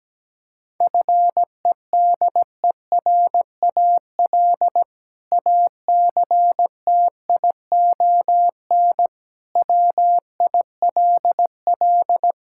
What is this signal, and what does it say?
Morse code, heard as a keyed tone, about 17 wpm, 700 Hz. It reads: FEDERAL ACTION WILL
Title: Sentences from Top 1000 Words - Review Part 5 - 17wpm